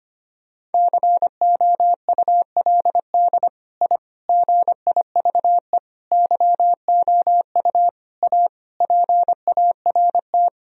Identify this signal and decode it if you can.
Morse code, heard as a keyed tone, about 25 wpm, 700 Hz. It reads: COULD I GIVE YOU A PART
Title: Sentences from Top 1000 Words - Review Part 1 - 25wpm